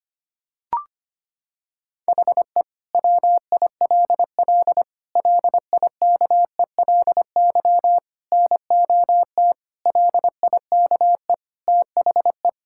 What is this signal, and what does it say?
Morse code, heard as a keyed tone, about 25 wpm, 700 Hz. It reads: HE WILL LIKELY NOT LIKE THE
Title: Sentences from Top 1000 Words - Review Part 5 - 25wpm